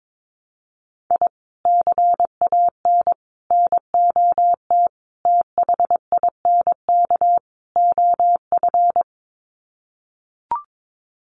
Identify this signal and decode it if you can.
Morse code, heard as a keyed tone, about 22 wpm, 700 Hz. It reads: I CAN NOT THINK OF
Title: Sets of 5 Words 22wpm